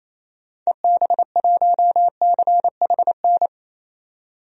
Morse code, heard as a keyed tone, about 28 wpm, 700 Hz. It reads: EB1CHN
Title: DXCC-100 Call Signs - 6 Characters - 28wpm